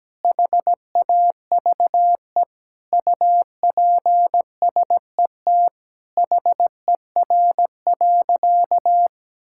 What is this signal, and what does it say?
Morse code, heard as a keyed tone, about 17 wpm, 700 Hz. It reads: HAVE UPSET HER.
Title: Sentences from Top 1000 Words - Review Part 5 - 17wpm